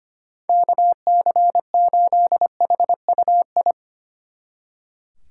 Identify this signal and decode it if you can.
Morse code, heard as a keyed tone, about 30 wpm, 700 Hz. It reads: KC8HUI
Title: Speed Racing - USA 2x3 Call Signs 25wpm